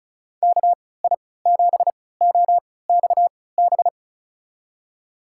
Morse code, heard as a keyed tone, about 36 wpm, spaced as for 20 wpm, 700 Hz. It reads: KI7OXB
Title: USA Call Signs - 6 Characters - 3x Character Spacing - 35wpm